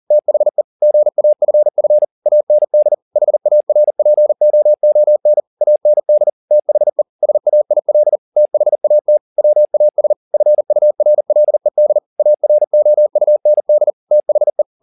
Morse code, heard as a keyed tone, about 40 wpm, 600 Hz. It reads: THE GAFF AND HARPOON AND THE SAIL THAT WAS FURLED AROUND THE